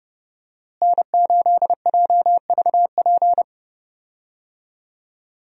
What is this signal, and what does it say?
Morse code, heard as a keyed tone, about 30 wpm, 700 Hz. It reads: N8JVP
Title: USA Call Signs - 5 Characters - Repeated 2x - 30wpm